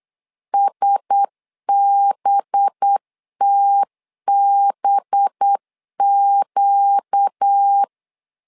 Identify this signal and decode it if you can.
Morse code, heard as a keyed tone, about 8 wpm, 800 Hz. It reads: SBTBQ
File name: LEZIONI / 5 LBVC / 5 LB 8wpm 2816